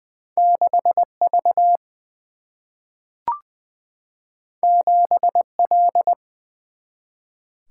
Morse code, heard as a keyed tone, about 21 wpm, 700 Hz. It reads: 6 V 7 L
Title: Single-Letter Single-Number 20wpm